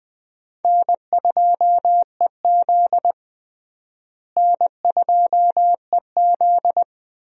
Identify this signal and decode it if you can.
Morse code, heard as a keyed tone, about 21 wpm, 700 Hz. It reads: N2EZ N2EZ
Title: USA Call Signs - 4 Characters - Repeated 3x - 20wpm